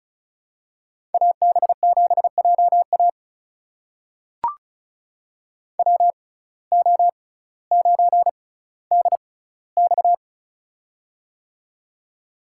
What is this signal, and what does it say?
Morse code, heard as a keyed tone, about 34 wpm, spaced as for 18 wpm, 700 Hz. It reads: AB7JA WO9DX
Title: USA Call Signs - 5 Characters - 6x Character Spacing - 35wpm